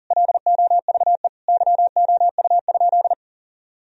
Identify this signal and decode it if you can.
Morse code, heard as a keyed tone, about 40 wpm, 700 Hz. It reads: LOVE YOU?
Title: Sentences from Top 1000 Words - Review Part 3 - 40wpm